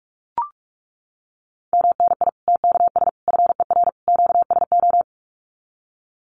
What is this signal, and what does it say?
Morse code, heard as a keyed tone, about 44 wpm, 700 Hz. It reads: MNI TKS FER QSO
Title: Common QSO Phrases 45wpm